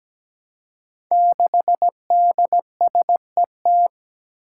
Morse code, heard as a keyed tone, about 17 wpm, 700 Hz. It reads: 6DSET
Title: Five Challenging Letter-Number Combinations 17wpm